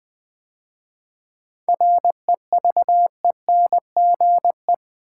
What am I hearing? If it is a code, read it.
Morse code, heard as a keyed tone, about 20 wpm, 700 Hz. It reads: REVENGE
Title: 7 Letter Words - First 3 Letters Given - 20wpm